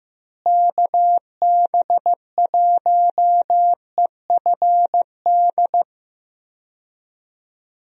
Morse code, heard as a keyed tone, about 15 wpm, 700 Hz. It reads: KB1EFD
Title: USA Call Signs - 6 Characters - Repeated 2x - 15wpm